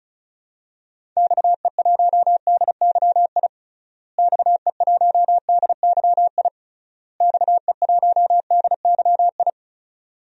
Morse code, heard as a keyed tone, about 36 wpm, 700 Hz. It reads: XE1DYI XE1DYI XE1DYI
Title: DXCC-100 Call Signs - 6 Characters - Repeated 3x - 35wpm